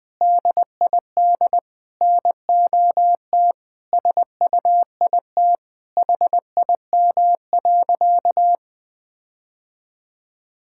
Morse code, heard as a keyed tone, about 20 wpm, 700 Hz. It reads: DID NOT SUIT HIM.
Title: Sentences from Top 1600 Words 20wpm